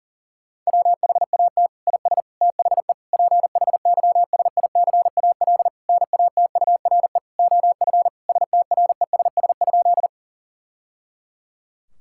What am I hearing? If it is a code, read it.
Morse code, heard as a keyed tone, about 40 wpm, 700 Hz. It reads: WHAT IS THE PHYSICAL NATURE OF STRESS?
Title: Sentences from Top 1000 Words - Review Part 6 - 40wpm